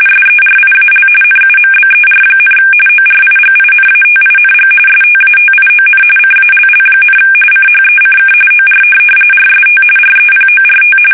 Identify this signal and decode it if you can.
RTTY, about 45.45 baud. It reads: SERIAL NUMBER, ORIGINATING STATION CALL, CHECK OR RST REPORT
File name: fsk850 / teletype-RTTY